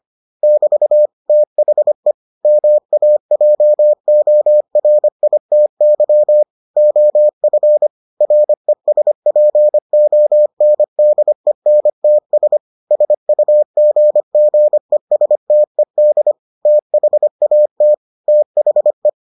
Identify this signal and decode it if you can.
Morse code, heard as a keyed tone, about 25 wpm, 600 Hz. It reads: <BT> THE MAJORITY OF RESPONDENTS SUGGESTED THAT THE